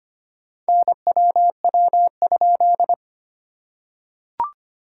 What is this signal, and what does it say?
Morse code, heard as a keyed tone, about 25 wpm, 690 Hz. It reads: NWW?
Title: SOTA QSO Practice 25wpm